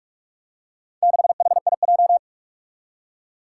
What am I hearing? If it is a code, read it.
Morse code, heard as a keyed tone, about 44 wpm, 700 Hz. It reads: 6HIJ
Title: Four Challenging Letter-Number Combinations 45wpm